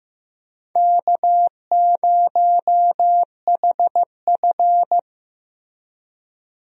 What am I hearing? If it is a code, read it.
Morse code, heard as a keyed tone, about 15 wpm, 700 Hz. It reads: K0HF
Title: USA Call Signs - 4 Characters - Repeated 2x - 15wpm